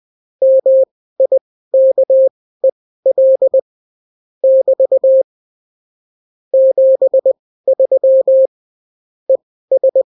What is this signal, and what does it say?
Morse code, heard as a keyed tone, about 20 wpm, spaced as for 14 wpm, 530 Hz. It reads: MIKEL <BT> 73 ES
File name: CW QSO libroCW4 20-10ppm